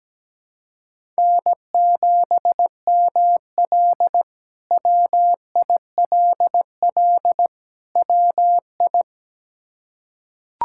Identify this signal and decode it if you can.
Morse code, heard as a keyed tone, about 17 wpm, 700 Hz. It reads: N7ML WILL WI
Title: K1USN SST QSO Practice - 17wpm